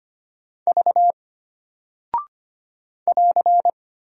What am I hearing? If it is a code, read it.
Morse code, heard as a keyed tone, about 25 wpm, 700 Hz. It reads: V <AR>
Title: Single Character (Letter:Number:Punctuation:Prosign) Alt 25wpm